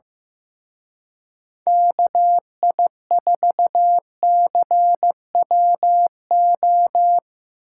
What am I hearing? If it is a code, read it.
Morse code, heard as a keyed tone, about 17 wpm, 700 Hz. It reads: KI4CWO
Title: Speed Racing - USA 2x3 Call Signs 15wpm